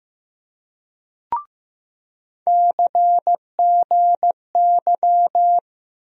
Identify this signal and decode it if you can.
Morse code, heard as a keyed tone, about 15 wpm, 700 Hz. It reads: CGY
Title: USA Call Sign Postfixes - 3 Letters 15wpm